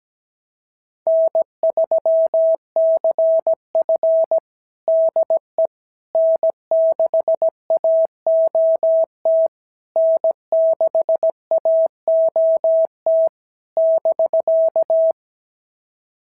Morse code, heard as a keyed tone, about 17 wpm, 650 Hz. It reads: N3CF DE N6AOT N6AOT <BK>
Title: SOTA QSO Practice 17wpm